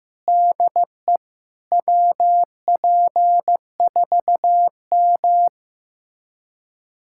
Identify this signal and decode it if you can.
Morse code, heard as a keyed tone, about 15 wpm, 700 Hz. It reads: DE WP4M
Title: USA Callsign DE Callsign - 4 Characters - 15wpm